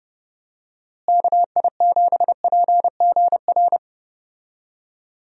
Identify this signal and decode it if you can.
Morse code, heard as a keyed tone, about 30 wpm, 700 Hz. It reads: KI7PGR
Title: USA Call Signs - 6 Characters - Repeated 3x - 30wpm